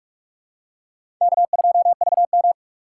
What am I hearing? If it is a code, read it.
Morse code, heard as a keyed tone, about 47 wpm, 700 Hz. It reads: K2VM
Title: USA Call Signs - 4 Characters - 3x Character Spacing - 45wpm